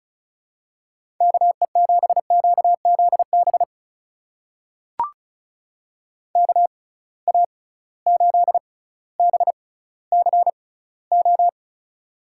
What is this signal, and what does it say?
Morse code, heard as a keyed tone, about 36 wpm, spaced as for 18 wpm, 700 Hz. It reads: KE7QZB KA8BCO
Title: USA Call Signs - 6 Characters - 6x Character Spacing - 35wpm